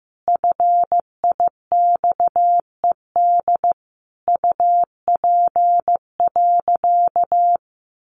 Morse code, heard as a keyed tone, about 15 wpm, 700 Hz. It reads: FIXED UP.
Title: Sentences from Top 1100 Words 15wpm